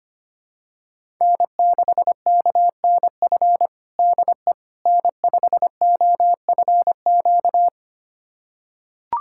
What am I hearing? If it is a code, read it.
Morse code, heard as a keyed tone, about 25 wpm, 700 Hz. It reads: N6KNF DE N5OFQ
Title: USA Callsign DE Callsign - 5 Characters - 25wpm